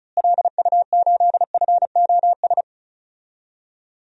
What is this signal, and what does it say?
Morse code, heard as a keyed tone, about 36 wpm, 700 Hz. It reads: LU8FOS
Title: DXCC-100 Call Signs - 6 Characters - Repeated 3x - 35wpm